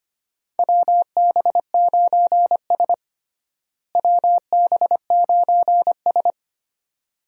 Morse code, heard as a keyed tone, about 25 wpm, 700 Hz. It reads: WB9S WB9S
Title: USA Call Signs - 4 Characters - Repeated 2x - 25wpm